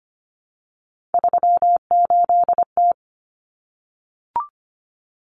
Morse code, heard as a keyed tone, about 25 wpm, 700 Hz. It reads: 38T
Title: Common Cut Numbers - 3-Digit 25wpm